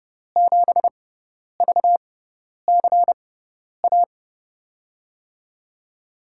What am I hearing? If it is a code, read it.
Morse code, heard as a keyed tone, about 30 wpm, spaced as for 10 wpm, 700 Hz. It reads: 7VCA
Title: USA Call Signs - 6 Characters - 6x Character Spacing - 30wpm